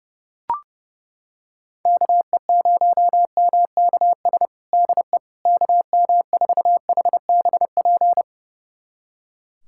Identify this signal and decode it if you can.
Morse code, heard as a keyed tone, about 30 wpm, 700 Hz. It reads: KE0MKS DE KM4HBP
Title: USA Callsign DE Callsign - 6 Characters - 30wpm